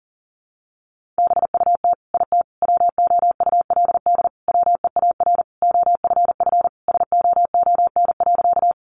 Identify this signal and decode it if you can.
Morse code, heard as a keyed tone, about 40 wpm, 700 Hz. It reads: BUT IT WOULD WEAR OFF SOON.
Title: Sentences from Top 1500 Words 40wpm